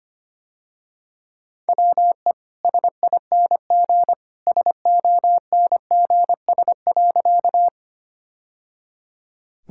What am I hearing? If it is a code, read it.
Morse code, heard as a keyed tone, about 25 wpm, 690 Hz. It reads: WE SING SONGS.
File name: Easy Sentences with Hints - 25wpm